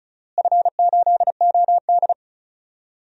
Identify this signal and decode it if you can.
Morse code, heard as a keyed tone, about 36 wpm, 690 Hz. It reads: F8OD
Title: DXCC-100 Call Signs - 4 Characters - Repeated 2x - 35wpm